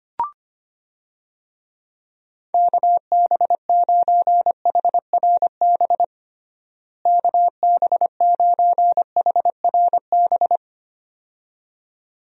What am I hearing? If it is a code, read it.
Morse code, heard as a keyed tone, about 25 wpm, 700 Hz. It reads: KB9HRB KB9HRB
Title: USA Call Signs - 6 Characters - Repeated 2x - 25wpm